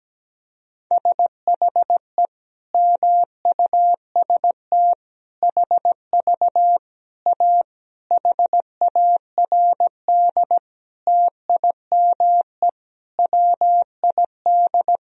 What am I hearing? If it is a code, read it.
Morse code, heard as a keyed tone, about 17 wpm, 700 Hz. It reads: SHE MUST HV A HARD TIME WID
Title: Sentences from Top 300 Words and Abbreviations 17wpm